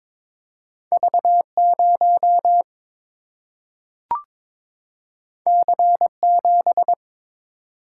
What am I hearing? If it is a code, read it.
Morse code, heard as a keyed tone, about 22 wpm, 700 Hz. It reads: V0 C7
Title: Single-Letter Single-Number 22wpm